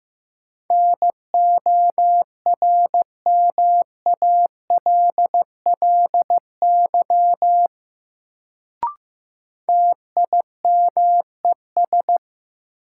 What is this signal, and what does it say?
Morse code, heard as a keyed tone, about 15 wpm, 700 Hz. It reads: NORMALLY TIMES
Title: Sentences from Top 1000 Words - Review Part 6 - New Word Drill - 15wpm